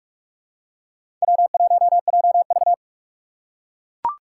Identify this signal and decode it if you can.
Morse code, heard as a keyed tone, about 43 wpm, 700 Hz. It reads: W1JV
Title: USA Call Signs - 4 Characters - 2x Character Spacing - 45wpm